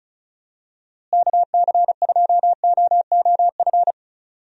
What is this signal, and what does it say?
Morse code, heard as a keyed tone, about 34 wpm, 700 Hz. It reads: KC2OOF
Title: USA Call Signs - 6 Characters - Repeated 3x - 35wpm